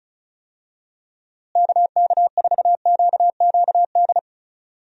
Morse code, heard as a keyed tone, about 34 wpm, 700 Hz. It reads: KK4QQD
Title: USA Call Signs - 6 Characters - Repeated 2x - 35wpm